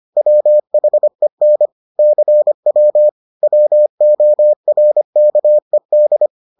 Morse code, heard as a keyed tone, about 25 wpm, 600 Hz. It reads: WHEN CW WORKED